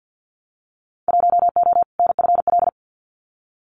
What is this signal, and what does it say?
Morse code, heard as a keyed tone, about 48 wpm, 700 Hz. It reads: 1O NFL
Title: Winter Field Day Exchanges 50wpm